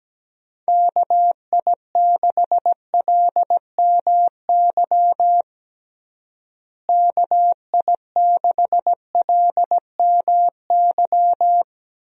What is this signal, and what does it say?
Morse code, heard as a keyed tone, about 17 wpm, 700 Hz. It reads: KI6LMY KI6LMY
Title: USA Call Signs - 6 Characters - Repeated 3x - 17wpm